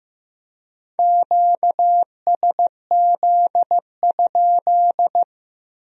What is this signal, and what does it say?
Morse code, heard as a keyed tone, about 18 wpm, 700 Hz. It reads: QSZ?
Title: Speed-Racing Q-Codes - 15wpm